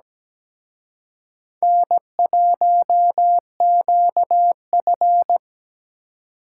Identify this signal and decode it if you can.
Morse code, heard as a keyed tone, about 20 wpm, 700 Hz. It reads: N1QF
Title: Speed Racing - USA 1x2 Call Signs 17wpm